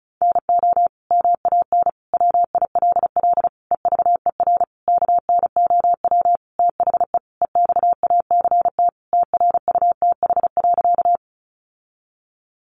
Morse code, heard as a keyed tone, about 34 wpm, 700 Hz. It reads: NO MAN WILL EVER KNOW THE EXACT TRUTH.
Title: Sentences from Top 1300 Words 35wpm